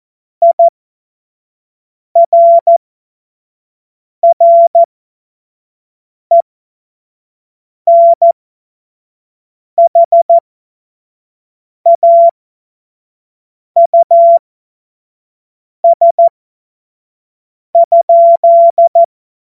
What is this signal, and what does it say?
Morse code, heard as a keyed tone, about 14 wpm, spaced as for 5 wpm, 680 Hz. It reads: IRRENHAUS?
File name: Morsekurs / Witze / Witz 29 April 2025 25 F70